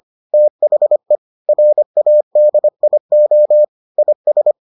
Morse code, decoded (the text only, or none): THE RADIO IS